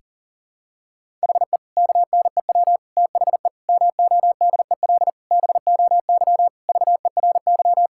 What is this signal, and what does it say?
Morse code, heard as a keyed tone, about 37 wpm, 700 Hz. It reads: HE KNEW THE MODEL BOY VERY
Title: Sentences from Top 1000 Words - Review Part 5 - 40wpm